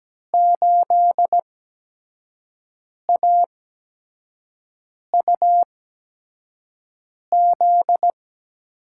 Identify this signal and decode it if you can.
Morse code, heard as a keyed tone, about 17 wpm, spaced as for 5 wpm, 700 Hz. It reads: 8AUZ
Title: USA Call Signs - 5 Characters - 8x Character Spacing - 17wpm